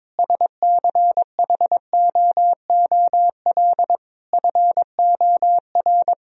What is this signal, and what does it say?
Morse code, heard as a keyed tone, about 22 wpm, 700 Hz. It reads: SCHOOL FOR
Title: Sentences from Top 1000 Words - Review Part 4 - 22wpm